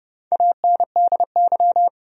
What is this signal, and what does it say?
Morse code, heard as a keyed tone, about 30 wpm, 700 Hz. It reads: ANDY
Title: USA Common Names - Max 4 Letters - 30wpm